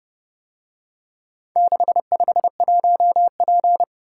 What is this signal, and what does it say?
Morse code, heard as a keyed tone, about 32 wpm, 700 Hz. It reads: 651P